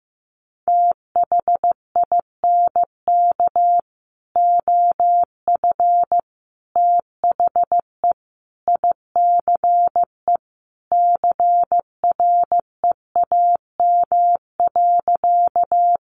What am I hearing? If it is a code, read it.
Morse code, heard as a keyed tone, about 15 wpm, 700 Hz. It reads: THINK OF THE ICE CREAM.
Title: Sentences from Top 1500 Words 15wpm